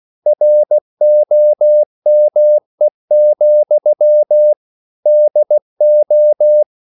Morse code, heard as a keyed tone, about 16 wpm, 600 Hz. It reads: ROME, DO
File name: CW/Practice